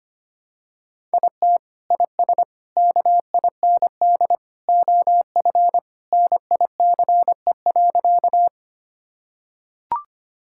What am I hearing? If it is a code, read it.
Morse code, heard as a keyed tone, about 25 wpm, 700 Hz. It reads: IT IS KIND OF NICE.
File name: practice / 25wpm / Sentences with 2-Word Phrases - 25wpm